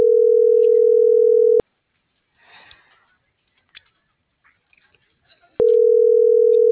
Ringback tone (North America), 440 + 480 Hz.